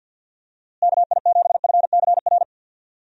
Morse code, heard as a keyed tone, about 49 wpm, 700 Hz. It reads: KI7FCR
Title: USA Call Signs - 6 Characters - Repeated 2x - 50wpm